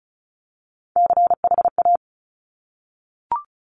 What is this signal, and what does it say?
Morse code, heard as a keyed tone, about 36 wpm, 700 Hz. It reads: CHA
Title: Top Three Letter Combinations in Words 35wpm